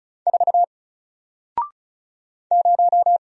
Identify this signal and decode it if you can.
Morse code, heard as a keyed tone, about 35 wpm, 700 Hz. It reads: 4 0
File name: Single-Letter-Number-35wpm